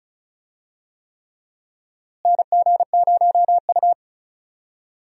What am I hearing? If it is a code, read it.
Morse code, heard as a keyed tone, about 34 wpm, 700 Hz. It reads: NG0U
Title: USA Call Signs - 4 Characters - Repeated 3x - 35wpm